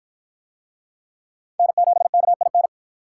Morse code, heard as a keyed tone, about 52 wpm, 700 Hz. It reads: N7KIN